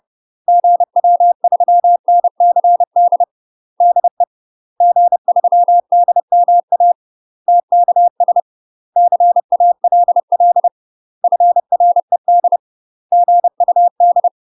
Morse code, heard as a keyed tone, about 30 wpm, 700 Hz. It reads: GW3NCD DE G3DMA TKS CALL FRED GUD
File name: UK MASTER QSOs TRAINING / 30WPM / UK MASTER QSOs 0018